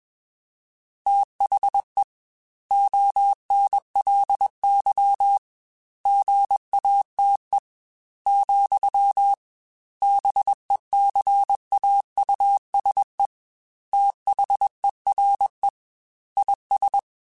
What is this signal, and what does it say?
Morse code, heard as a keyed tone, about 21 wpm, 800 Hz. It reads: THE ONLY GATE, BECAUSE THERE IS
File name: Technical Subjects / Technical Subjects-mp3s / AND Gate @20